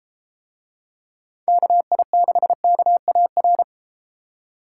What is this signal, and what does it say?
Morse code, heard as a keyed tone, about 33 wpm, 700 Hz. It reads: KI6KAR